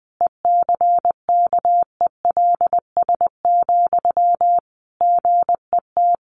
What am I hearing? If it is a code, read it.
Morse code, heard as a keyed tone, about 20 wpm, 700 Hz. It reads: ECKELS, GET
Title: A Sound Of Thunder 20wpm